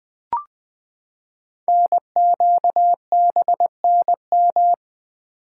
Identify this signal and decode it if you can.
Morse code, heard as a keyed tone, about 20 wpm, 700 Hz. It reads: NQBNM
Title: Code Groups - 5 Letters - 20wpm